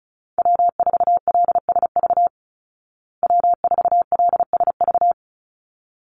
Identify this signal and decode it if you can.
Morse code, heard as a keyed tone, about 34 wpm, 700 Hz. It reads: W4LSV W4LSV
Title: Call Signs Repeated 3x - 35wpm